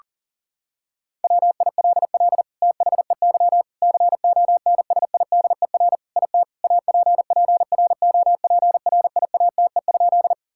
Morse code, heard as a keyed tone, about 40 wpm, 700 Hz. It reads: WILL THEY CONSIDER IT APPROPRIATE?
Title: Sentences from Top 1000 Words - Review Part 5 - 40wpm